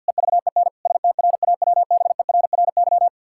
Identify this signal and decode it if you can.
Morse code, heard as a keyed tone, about 50 wpm, 700 Hz. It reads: EVEN STRAWBERRY
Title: Pangram Challenge v2 - 50wpm